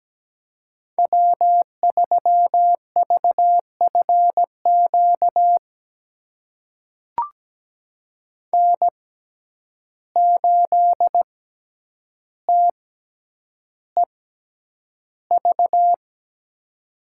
Morse code, heard as a keyed tone, about 17 wpm, spaced as for 9 wpm, 700 Hz. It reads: W3VFQ N8TEV
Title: USA Call Signs - 5 Characters - 6x Character Spacing - 17wpm